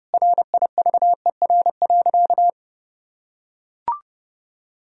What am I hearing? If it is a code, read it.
Morse code, heard as a keyed tone, about 30 wpm, 700 Hz. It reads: RIVER.
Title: Sentences from Top 1000 Words - Review Part 4 - 30wpm